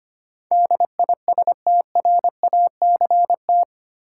Morse code, heard as a keyed tone, about 25 wpm, 700 Hz. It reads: DISTRACT
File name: Eight Letter Words - 25wpm